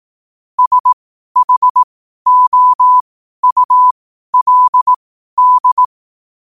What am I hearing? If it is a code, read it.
Morse code, heard as a keyed tone, about 18 wpm, spaced as for 12 wpm, 1000 Hz. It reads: SHOULD
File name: qotdfiles / 12wpm